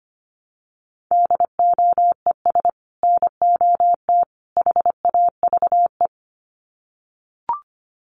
Morse code, heard as a keyed tone, about 25 wpm, 700 Hz. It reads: DOES NOT HAVE
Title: Sets of 3 Words 25wpm